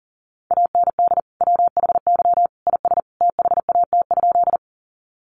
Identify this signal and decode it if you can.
Morse code, heard as a keyed tone, about 40 wpm, 700 Hz. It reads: AND WHY IS THAT?
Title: Sentences from Top 200 Words 40wpm